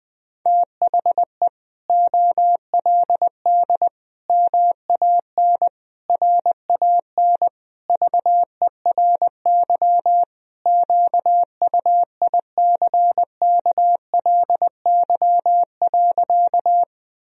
Morse code, heard as a keyed tone, about 21 wpm, 700 Hz. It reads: THE OLD MAN RAN VERY QUICKLY.
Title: Sentences from Top 1000 Words - Review Part 3 - 20wpm